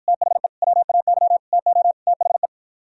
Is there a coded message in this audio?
Yes: Morse code from 0.1 to 2.5 s, heard as a keyed tone, about 52 wpm, 700 Hz. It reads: THE WAY TO THE